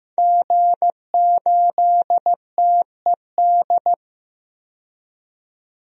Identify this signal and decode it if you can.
Morse code, heard as a keyed tone, about 15 wpm, 700 Hz. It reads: G8TED
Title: DXCC-100 Call Signs - 5 Characters - 15wpm